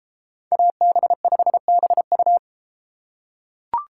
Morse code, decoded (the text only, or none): AB5BU